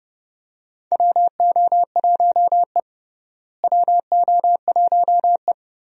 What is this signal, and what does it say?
Morse code, heard as a keyed tone, about 31 wpm, 700 Hz. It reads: WO1E WO1E
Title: USA Call Signs - 4 Characters - Repeated 2x - 30wpm